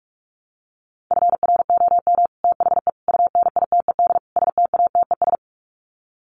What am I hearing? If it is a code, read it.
Morse code, heard as a keyed tone, about 44 wpm, 700 Hz. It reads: FROM THE UNITED STATES
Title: Sets of 4 Words 45wpm